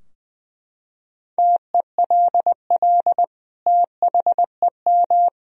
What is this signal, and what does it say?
Morse code, heard as a keyed tone, about 20 wpm, 700 Hz. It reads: TELL THEM
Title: Sentences from Top 1000 Words - Review Part 3 - 20wpm